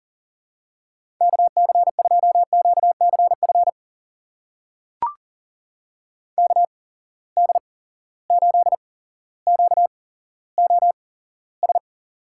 Morse code, heard as a keyed tone, about 40 wpm, spaced as for 17 wpm, 700 Hz. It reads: KC2QCF KD8QOS
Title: USA Call Signs - 6 Characters - 8x Character Spacing - 40wpm